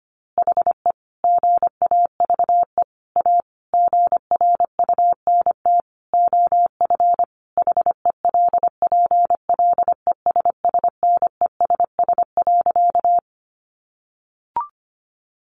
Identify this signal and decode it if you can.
Morse code, heard as a keyed tone, about 25 wpm, 700 Hz. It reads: HE GAVE A GRUNT OF HELPLESSNESS.
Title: A Sound Of Thunder 25wpm